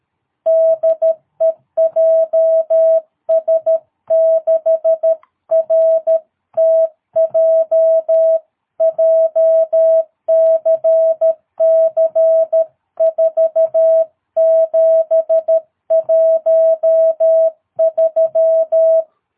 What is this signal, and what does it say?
Morse code, heard as a keyed tone, about 13 wpm, 660 Hz. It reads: DE JS6RTJ JCC 4713
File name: jh8cbh / 2022cw